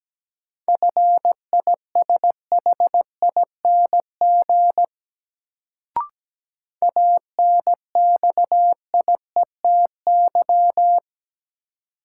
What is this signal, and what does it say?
Morse code, heard as a keyed tone, about 17 wpm, 700 Hz. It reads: FISHING ANXIETY
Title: Sentences from Top 1600 Words - New Word Drill Alt - 17wpm